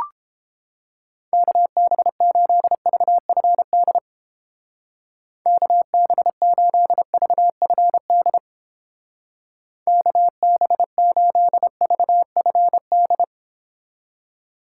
Morse code, heard as a keyed tone, about 29 wpm, 700 Hz. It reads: KB8VFD KB8VFD KB8VFD